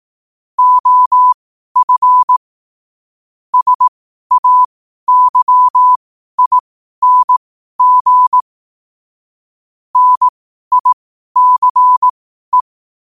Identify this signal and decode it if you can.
Morse code, heard as a keyed tone, about 18 wpm, spaced as for 12 wpm, 1000 Hz. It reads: OF SAYING NICE